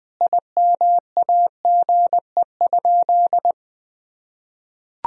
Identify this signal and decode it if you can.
Morse code, heard as a keyed tone, about 21 wpm, 700 Hz. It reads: IMAGE?
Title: Sentences from Top 1000 Words - Review Part 4 - 20wpm